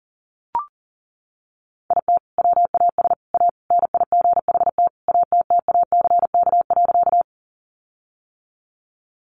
Morse code, heard as a keyed tone, about 38 wpm, 700 Hz. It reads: IT WAS A NIGHT ATTACK.
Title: Sentences from Top 1500 Words 40wpm